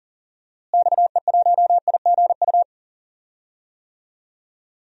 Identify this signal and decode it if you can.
Morse code, heard as a keyed tone, about 40 wpm, 700 Hz. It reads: XE1IGU
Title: DXCC-100 Call Signs - 6 Characters - Repeated 3x - 40wpm